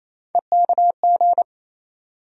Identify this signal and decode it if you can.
Morse code, heard as a keyed tone, about 27 wpm, 700 Hz. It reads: EKG
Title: USA Call Sign Postfixes - 3 Letters 28wpm